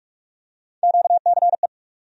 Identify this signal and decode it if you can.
Morse code, heard as a keyed tone, about 46 wpm, 700 Hz. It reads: QCE
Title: USA Call Sign Postfixes - 3 Letters 45wpm